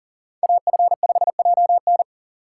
Morse code, heard as a keyed tone, about 40 wpm, 700 Hz. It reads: AF5JN
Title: USA Call Signs - 5 Characters - 6x Character Spacing - 40wpm